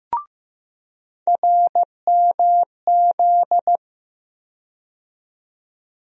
Morse code, heard as a keyed tone, about 16 wpm, 700 Hz. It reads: RMZ
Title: USA Call Sign Postfixes - 3 Letters 15wpm